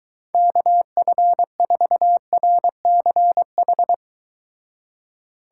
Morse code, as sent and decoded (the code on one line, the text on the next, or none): -.- ..-. ....- .-. -.-. ....
KF4RCH